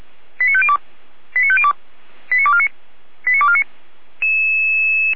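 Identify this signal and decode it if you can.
Five-tone selective call (ZVEI), heard as quick runs of five tones, tones of about 70 ms. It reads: 89502 89502 89249 89249
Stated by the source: ZVEI five-tone call 89250 received on SDR, Feuerwehr Orlamünde.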